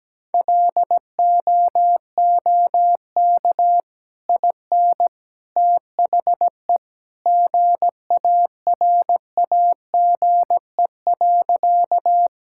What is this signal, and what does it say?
Morse code, heard as a keyed tone, about 17 wpm, 700 Hz. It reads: LOOK IN THE GARAGE.
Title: Sentences from Top 1600 Words 17wpm